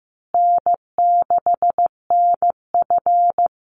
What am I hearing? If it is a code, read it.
Morse code, heard as a keyed tone, about 15 wpm, 700 Hz. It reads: N6NF
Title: Call Signs Repeated 2x - 15wpm